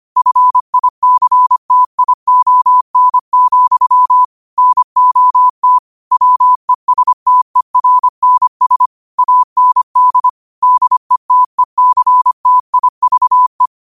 Morse code, heard as a keyed tone, about 25 wpm, 1000 Hz. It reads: FICTION, NOT WESTERNS AND DETECTIVE
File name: qotdfiles / 25wpm